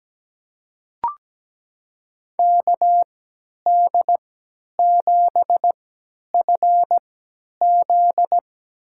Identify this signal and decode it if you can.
Morse code, heard as a keyed tone, about 17 wpm, spaced as for 10 wpm, 700 Hz. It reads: KD7FZ
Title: USA Call Signs - 3x Character Spacing - 17wpm